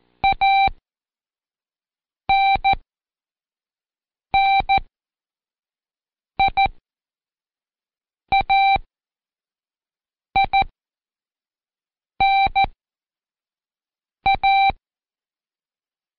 Morse code, heard as a keyed tone, about 14 wpm, spaced as for 5 wpm, 780 Hz. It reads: ANNIAINA
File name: Code Quick / AudioDownld1 / Seg42-Day5-Intro F